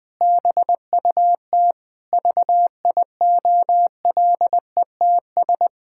Morse code, heard as a keyed tone, about 20 wpm, 700 Hz. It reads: BUT VIOLETS
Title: Pangram Challenge v2 - 20wpm